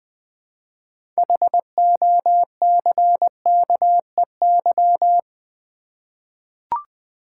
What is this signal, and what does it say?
Morse code, heard as a keyed tone, about 20 wpm, 700 Hz. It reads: HOCKEY
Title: 6 Letter Words - First 3 Letters Given - 20wpm